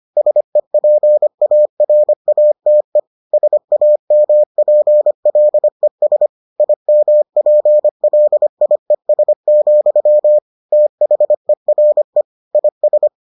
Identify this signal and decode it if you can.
Morse code, heard as a keyed tone, about 25 wpm, 600 Hz. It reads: SEPARATE SAMPLES IMPLIES, THERE IS